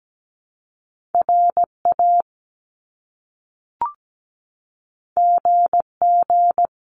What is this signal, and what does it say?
Morse code, heard as a keyed tone, about 17 wpm, 700 Hz. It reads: RA GG
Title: Top Two Letter Combinations in Words 17wpm